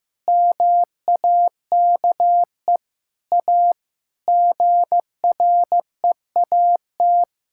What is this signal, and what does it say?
Morse code, heard as a keyed tone, about 15 wpm, 700 Hz. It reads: MAKE A GREAT
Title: Sentences from Top 1000 Words - Review Part 5 - 15wpm